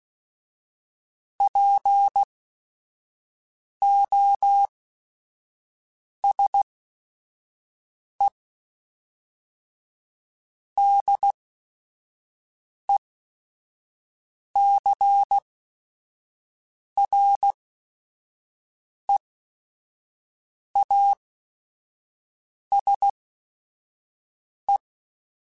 Morse code, heard as a keyed tone, about 16 wpm, spaced as for 5 wpm, 800 Hz. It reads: POSE DECREASE
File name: Technical Subjects / Technical Subjects-mp3s / XL @5